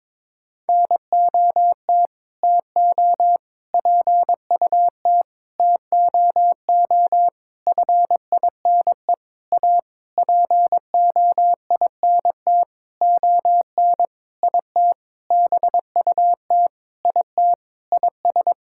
Morse code, heard as a keyed tone, about 22 wpm, 700 Hz. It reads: NOT TO PUT TOO FINE A POINT ON IT BUT IT IS
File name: Sentences with 9-Word Phrases - 22wpm